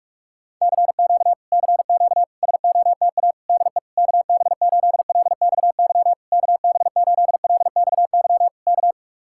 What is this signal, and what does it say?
Morse code, heard as a keyed tone, about 49 wpm, 700 Hz. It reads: CQ CQ SOTA DE KB8LXY KB8LXY K